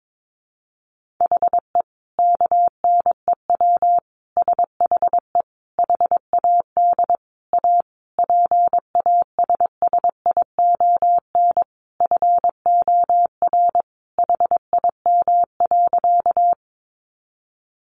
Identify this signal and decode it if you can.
Morse code, heard as a keyed tone, about 22 wpm, 700 Hz. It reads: HE KNEW SHE HAD A PASSION FOR HIM.
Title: Sentences from Top 1300 Words 22wpm